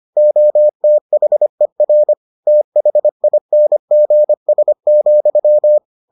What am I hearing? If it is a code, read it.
Morse code, heard as a keyed tone, about 25 wpm, 600 Hz. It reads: OTHER THINGS,